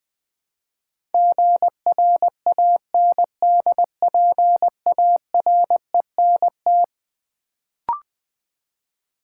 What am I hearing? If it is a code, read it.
Morse code, heard as a keyed tone, about 20 wpm, 700 Hz. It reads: GRANDPARENT
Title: Build A Word - 4 Letters At A Time - 11-Letter Words - 20wpm